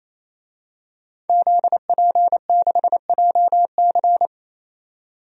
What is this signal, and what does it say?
Morse code, heard as a keyed tone, about 28 wpm, 700 Hz. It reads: ZP6JC
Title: DXCC-100 Call Signs - 5 Characters - 28wpm